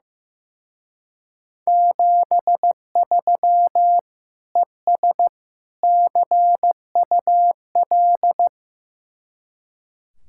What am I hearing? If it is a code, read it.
Morse code, heard as a keyed tone, about 15 wpm, 700 Hz. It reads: 73 ES CUL
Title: Speed-Racing Common QSO Phrases Encore - 15wpm